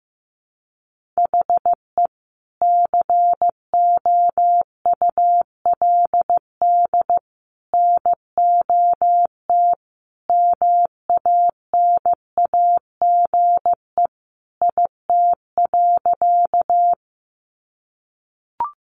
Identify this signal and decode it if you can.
Morse code, heard as a keyed tone, about 15 wpm, 700 Hz. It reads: HE COULD NOT MANAGE IT.
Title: Sentences from Top 1400 Words 15wpm